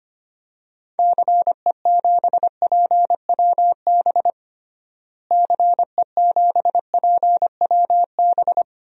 Morse code, heard as a keyed tone, about 25 wpm, 700 Hz. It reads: CE7PWB CE7PWB
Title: DXCC-100 Call Signs - 6 Characters - Repeated 2x - 25wpm